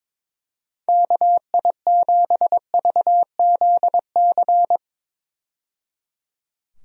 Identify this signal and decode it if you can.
Morse code, heard as a keyed tone, about 22 wpm, 700 Hz. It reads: KI7VZC
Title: USA Call Signs - 6 Characters - 22wpm